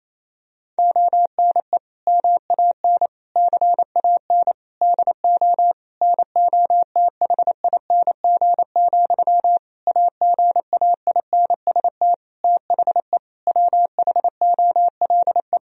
Morse code, heard as a keyed tone, about 28 wpm, 700 Hz. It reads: ONE MAN CAN DO NOTHING, AGAINST THE WHOLE
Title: Sentences from Top 1000 Words - Review Part 4 - 28wpm